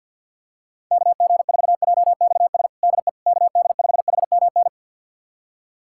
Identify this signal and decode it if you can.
Morse code, heard as a keyed tone, about 45 wpm, 700 Hz. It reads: KG4JXS DE KD5HMN
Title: USA Callsign DE Callsign - 6 Characters - 50wpm